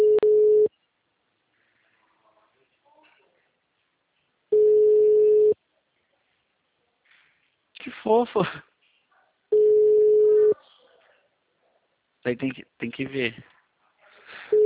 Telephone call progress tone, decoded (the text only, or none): ringback tone